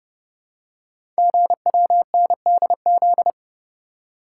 Morse code, heard as a keyed tone, about 30 wpm, 700 Hz. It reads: GWNDZ
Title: Code Groups - 5 Letters - 30wpm